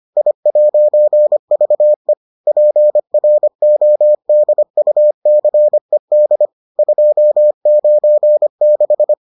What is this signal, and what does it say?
Morse code, heard as a keyed tone, about 25 wpm, 600 Hz. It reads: I'VE PRODUCED 296